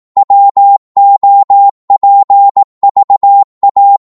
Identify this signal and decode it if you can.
Morse code, heard as a keyed tone, about 18 wpm, 800 Hz. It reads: WOPVA
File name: Morse / ltr-18wpm